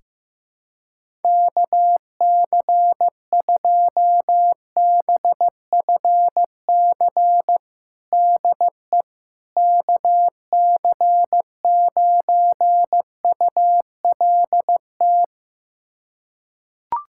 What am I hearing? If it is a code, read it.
Morse code, heard as a keyed tone, about 15 wpm, 700 Hz. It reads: KC2BFC DE KC9ULT
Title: USA Callsign DE Callsign - 6 Characters - 15wpm